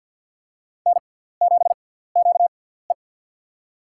Morse code, heard as a keyed tone, about 47 wpm, spaced as for 17 wpm, 700 Hz. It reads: N7QE
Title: USA Call Signs - 4 Characters - 6x Character Spacing - 50wpm